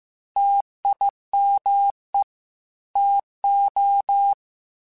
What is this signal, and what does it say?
Morse code, heard as a keyed tone, about 15 wpm, 800 Hz. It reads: TIME TO
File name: Technical Subjects / Technical Subjects-mp3s / AND Gate @13